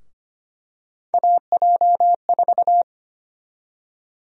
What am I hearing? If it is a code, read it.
Morse code, heard as a keyed tone, about 25 wpm, 700 Hz. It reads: AJ4
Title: USA Call Sign Prefixes - 25wpm